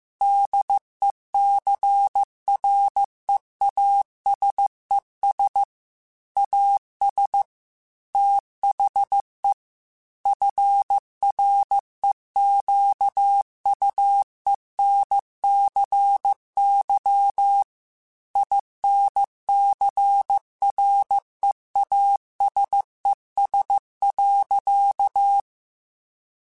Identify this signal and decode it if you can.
Morse code, heard as a keyed tone, about 15 wpm, 800 Hz. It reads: DECREASES AS THE FREQUENCY INCREASES.